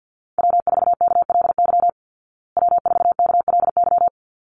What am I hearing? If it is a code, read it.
Morse code, heard as a keyed tone, about 50 wpm, 700 Hz. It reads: W4KLY W4KLY
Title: Call Signs Repeated 2x - 50wpm